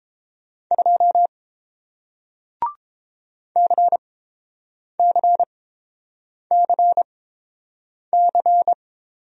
Morse code, heard as a keyed tone, about 30 wpm, spaced as for 7 wpm, 700 Hz. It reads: 2 CCCC